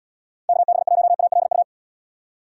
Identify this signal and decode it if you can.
Morse code, heard as a keyed tone, about 76 wpm, 700 Hz. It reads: KB1ALU